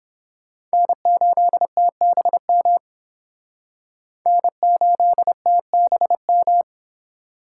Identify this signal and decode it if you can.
Morse code, heard as a keyed tone, about 30 wpm, 700 Hz. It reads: N8TBM N8TBM